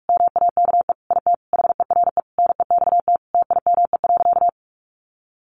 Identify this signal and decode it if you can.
Morse code, heard as a keyed tone, about 44 wpm, 700 Hz. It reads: MAKE IT HERE NEXT TIME.
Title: Sentences from Top 1500 Words 45wpm